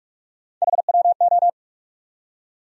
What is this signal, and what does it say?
Morse code, heard as a keyed tone, about 46 wpm, 700 Hz. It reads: HWO
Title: USA Call Sign Postfixes - 3 Letters 45wpm